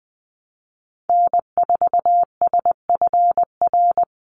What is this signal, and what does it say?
Morse code, heard as a keyed tone, about 19 wpm, 700 Hz. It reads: N4SFR
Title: NA SOTA Activator Callsigns 20wpm